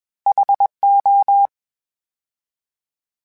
Morse code, heard as a keyed tone, about 21 wpm, 800 Hz. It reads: HO